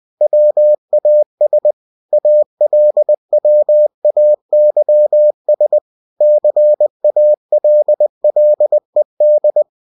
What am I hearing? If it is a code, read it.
Morse code, heard as a keyed tone, about 20 wpm, 600 Hz. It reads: WAS ALWAYS CALLED